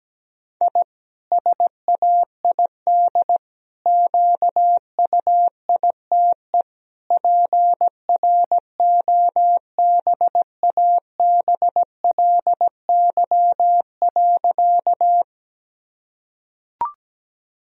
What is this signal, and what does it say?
Morse code, heard as a keyed tone, about 17 wpm, 700 Hz. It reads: I SAID QUITE PROBABLY.
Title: Sentences from Top 1000 Words - Review Part 4 - 17wpm